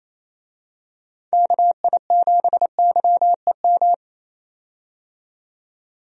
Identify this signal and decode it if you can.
Morse code, heard as a keyed tone, about 27 wpm, 700 Hz. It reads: KI7YEM
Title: USA Call Signs - 6 Characters - 28wpm